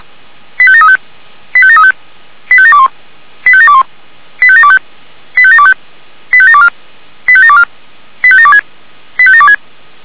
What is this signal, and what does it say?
Five-tone selective call (ZVEI), heard as quick runs of five tones, tones of about 70 ms. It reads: 85735 85735 85721 85721 85725 85725 85724 85724 85726 85726